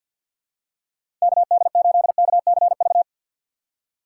Morse code, heard as a keyed tone, about 52 wpm, 700 Hz. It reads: KD8KCV
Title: USA Call Signs - 2x Character Spacing - 50wpm